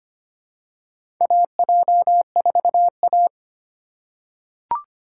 Morse code, heard as a keyed tone, about 25 wpm, 700 Hz. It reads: AJ4A
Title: USA Call Signs - 4 Characters - 3x Character Spacing - 25wpm